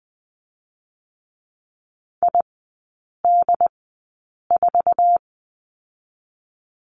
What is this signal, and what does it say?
Morse code, heard as a keyed tone, about 20 wpm, spaced as for 8 wpm, 700 Hz. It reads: ID4
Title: 067 Lesson 12e - Letter d - Three Characters 20wpm